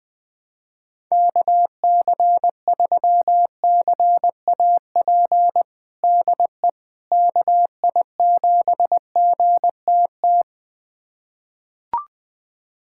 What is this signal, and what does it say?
Morse code, heard as a keyed tone, about 20 wpm, 700 Hz. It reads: KC3CAP DE KI7GTT
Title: USA Callsign DE Callsign 20wpm